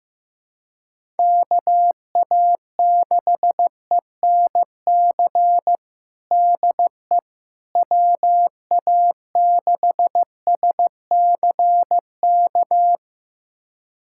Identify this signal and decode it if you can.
Morse code, heard as a keyed tone, about 15 wpm, 700 Hz. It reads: KA6ENC DE WA6SCK
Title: USA Callsign DE Callsign - 6 Characters - 15wpm